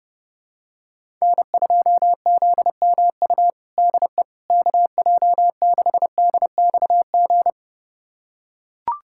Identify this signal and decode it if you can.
Morse code, heard as a keyed tone, about 30 wpm, 700 Hz. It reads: N2ZMU DE KJ6DXG
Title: USA Callsign DE Callsign 30wpm